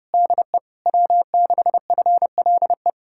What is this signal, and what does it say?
Morse code, heard as a keyed tone, about 33 wpm, 700 Hz. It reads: DE W6FLE